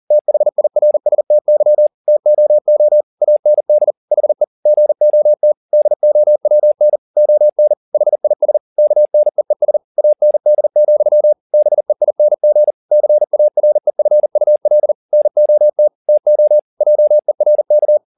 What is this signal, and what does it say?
Morse code, heard as a keyed tone, about 40 wpm, 600 Hz. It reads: THIRSTY TOO AND HE GOT DOWN ON HIS KNEES AND, BEING CAREFUL NOT TO JERK